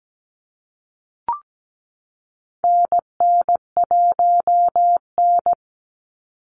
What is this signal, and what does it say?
Morse code, heard as a keyed tone, about 17 wpm, 700 Hz. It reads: NN1N
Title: Call Signs Repeated 3x - 17wpm